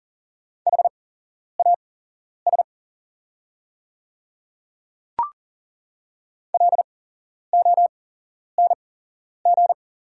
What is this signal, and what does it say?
Morse code, heard as a keyed tone, about 40 wpm, spaced as for 11 wpm, 690 Hz. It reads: HAS LONG
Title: 8x Character Spacing - Top 100 Words - 40wpm